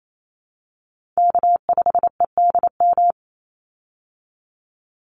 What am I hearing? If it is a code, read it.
Morse code, heard as a keyed tone, about 28 wpm, 700 Hz. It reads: K5EDM
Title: Call Signs Repeated 3x - 28wpm